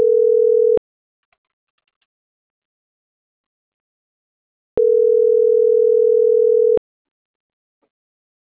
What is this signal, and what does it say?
Ringback tone (North America), 440 + 480 Hz.